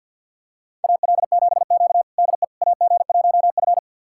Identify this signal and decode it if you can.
Morse code, heard as a keyed tone, about 51 wpm, 700 Hz. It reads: AL7Q DE AG1F
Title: USA Callsign DE Callsign - 4 Characters - 50wpm